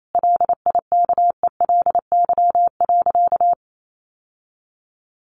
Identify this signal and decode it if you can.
Morse code, heard as a keyed tone, about 28 wpm, 700 Hz. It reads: LIKELY.
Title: Sentences from Top 600 Words 28wpm